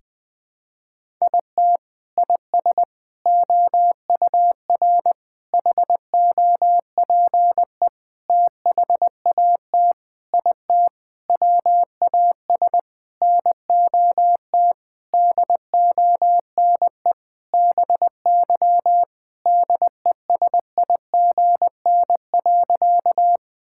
Morse code, heard as a keyed tone, about 20 wpm, 700 Hz. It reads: IT IS OUR HOPE THAT IT WAS NOT DONE BY DESIGN.
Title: Sentences from Top 1000 Words - Review Part 5 - 20wpm